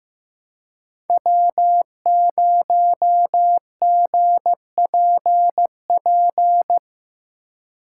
Morse code, heard as a keyed tone, about 15 wpm, 700 Hz. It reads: W0GPP
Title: USA Call Signs - 5 Characters - 3x Character Spacing - 15wpm